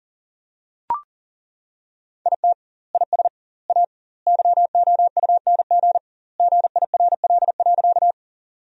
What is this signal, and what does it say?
Morse code, heard as a keyed tone, about 37 wpm, 700 Hz. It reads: IT IS A YOUNG GIRL.
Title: 2x Word-Spacing - Sentences from Top 400 Words - 40wpm